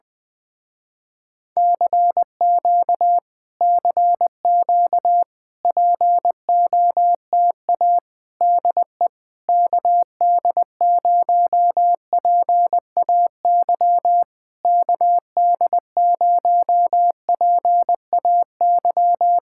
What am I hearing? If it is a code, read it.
Morse code, heard as a keyed tone, about 20 wpm, 700 Hz. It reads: CQ CQ POTA DE KD0PAY KD0PAY
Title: Speed-Racing Common QSO Phrases - 17wpm